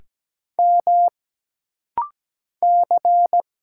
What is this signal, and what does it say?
Morse code, heard as a keyed tone, about 17 wpm, 700 Hz. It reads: M C
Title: Single Letters - Rapid-Fire 17wpm